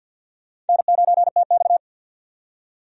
Morse code, heard as a keyed tone, about 52 wpm, 700 Hz. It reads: N9TX
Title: USA Call Signs - 4 Characters - 50wpm